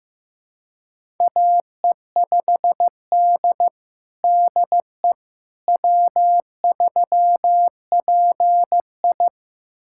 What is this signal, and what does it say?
Morse code, heard as a keyed tone, about 16 wpm, 700 Hz. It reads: AE5D DE W3PI
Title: USA Callsign DE Callsign - 4 Characters - 15wpm